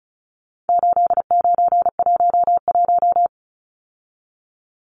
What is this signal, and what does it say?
Morse code, heard as a keyed tone, about 34 wpm, 700 Hz. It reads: 8911
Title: Numbers - 4-Digit 35wpm